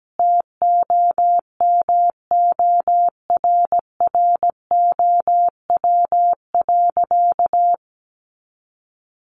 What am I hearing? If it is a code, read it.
Morse code, heard as a keyed tone, about 17 wpm, 700 Hz. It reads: TOMORROW.
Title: Sentences from Top 1100 Words 17wpm